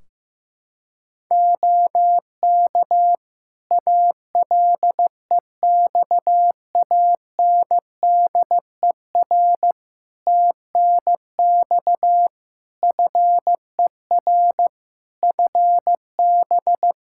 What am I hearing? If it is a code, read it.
Morse code, heard as a keyed tone, about 15 wpm, 700 Hz. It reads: OK ALEXANDER TNX FER FB
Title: K1USN SST QSO Practice 15wpm